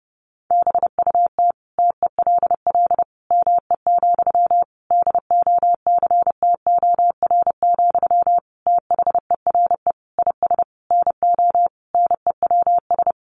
Morse code, heard as a keyed tone, about 30 wpm, 700 Hz. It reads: BUT TELL ME, DOCTOR, THERE IS NO NEWS
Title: Sentences from Top 1500 Words 30wpm